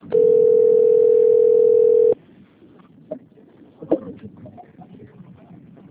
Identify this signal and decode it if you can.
Ringback tone (North America), 440 + 480 Hz.
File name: rg-1600-unknown- - - .31722